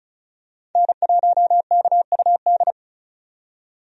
Morse code, heard as a keyed tone, about 34 wpm, 700 Hz. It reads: N1KUD
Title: USA Call Signs - 5 Characters - 35wpm